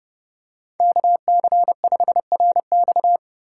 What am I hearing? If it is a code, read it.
Morse code, heard as a keyed tone, about 31 wpm, 700 Hz. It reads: KC5RX